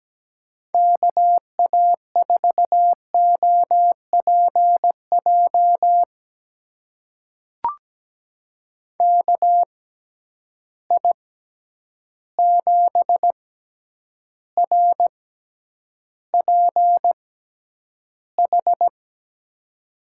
Morse code, heard as a keyed tone, about 17 wpm, spaced as for 9 wpm, 700 Hz. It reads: KA4OPJ KI7RPH
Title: USA Call Signs - 6 Characters - 6x Character Spacing - 17wpm